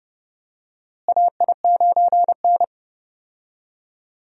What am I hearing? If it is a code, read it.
Morse code, heard as a keyed tone, about 32 wpm, 700 Hz. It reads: AI9N
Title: USA Call Signs - 4 Characters - Repeated 3x - 30wpm